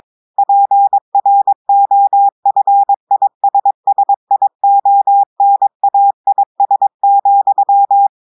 Morse code, heard as a keyed tone, about 22 wpm, 800 Hz. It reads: PROFISSIONAIS,
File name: audios / portugais / portugais texte en clair 1 vitesse 22 mots minutes